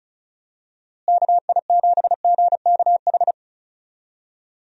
Morse code, heard as a keyed tone, about 36 wpm, 700 Hz. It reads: KI7GKH
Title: USA Call Signs - 6 Characters - Repeated 2x - 35wpm